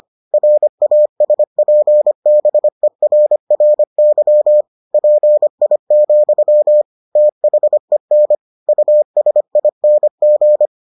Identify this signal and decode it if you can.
Morse code, heard as a keyed tone, about 25 wpm, 600 Hz. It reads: RASPBERRY PI, THEN USING